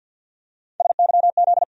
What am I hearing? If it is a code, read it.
Morse code, heard as a keyed tone, about 53 wpm, 700 Hz. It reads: SYZ
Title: USA Call Sign Postfixes - 3 Letters 50wpm